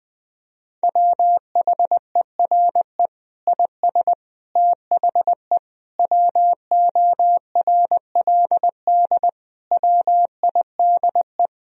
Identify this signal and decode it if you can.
Morse code, heard as a keyed tone, about 21 wpm, 700 Hz. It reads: WHERE IS THE WORLD WIDE
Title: Sentences from Top 1000 Words - Review Part 4 - 20wpm